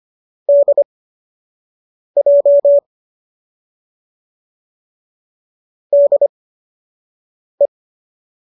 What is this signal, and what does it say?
Morse code, heard as a keyed tone, about 25 wpm, spaced as for 6 wpm, 570 Hz. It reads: DJ DE